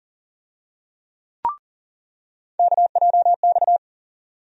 Morse code, heard as a keyed tone, about 40 wpm, 700 Hz. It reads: KJX
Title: USA Call Sign Postfixes - 3 Letters 40wpm